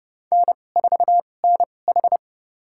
Morse code, heard as a keyed tone, about 32 wpm, spaced as for 22 wpm, 700 Hz. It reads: N4NH
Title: USA Call Signs - 4 Characters - 2x Character Spacing - 30wpm